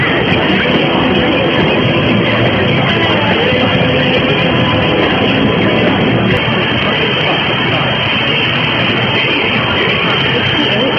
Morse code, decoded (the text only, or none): none